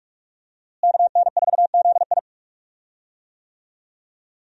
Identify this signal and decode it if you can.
Morse code, heard as a keyed tone, about 43 wpm, 700 Hz. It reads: KN4ZI
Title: USA Call Signs - 5 Characters - Repeated 2x - 45wpm